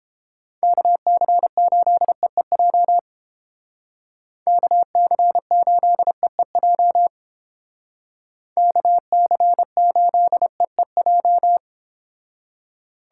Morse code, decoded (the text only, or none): KC8EEJ KC8EEJ KC8EEJ